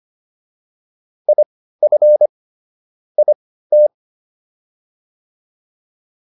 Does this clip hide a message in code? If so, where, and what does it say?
Yes: Morse code from 1.3 to 3.9 s, heard as a keyed tone, about 25 wpm, spaced as for 15 wpm, 600 Hz. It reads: IF IT